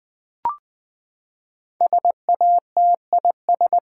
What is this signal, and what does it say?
Morse code, heard as a keyed tone, about 20 wpm, 700 Hz. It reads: SATIS
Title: Build A Word - 5 Letters At A Time - 12-Letter Words - 20wpm